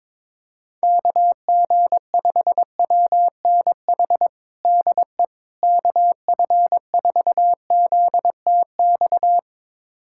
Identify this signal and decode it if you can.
Morse code, heard as a keyed tone, about 22 wpm, 700 Hz. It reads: KG5WNH DE KF4ZTX
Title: USA Callsign DE Callsign 22wpm